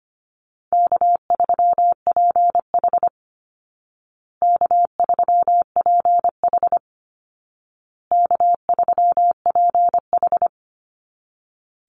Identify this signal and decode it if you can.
Morse code, heard as a keyed tone, about 25 wpm, 700 Hz. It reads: K3PH K3PH K3PH
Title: Call Signs Repeated 2x - 25wpm